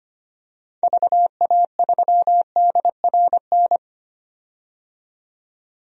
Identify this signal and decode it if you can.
Morse code, heard as a keyed tone, about 25 wpm, 700 Hz. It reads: VA3DRN
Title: DXCC-100 Call Signs - 6 Characters - 25wpm